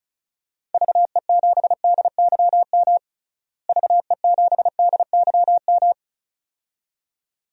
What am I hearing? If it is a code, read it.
Morse code, heard as a keyed tone, about 36 wpm, 700 Hz. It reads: VE7DYM VE7DYM
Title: DXCC-100 Call Signs - 6 Characters - Repeated 2x - 35wpm